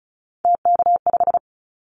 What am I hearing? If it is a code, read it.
Morse code, heard as a keyed tone, about 36 wpm, 700 Hz. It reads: TK5
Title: Callsign Prefixes for Top 100 DXCC - 35wpm